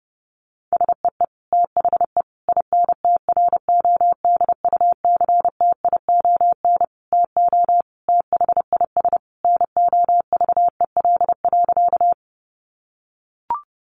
Morse code, heard as a keyed tone, about 30 wpm, 700 Hz. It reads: SEE THE INTRODUCTION TO THIS NOVEL.
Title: Sentences from Top 1300 Words 30wpm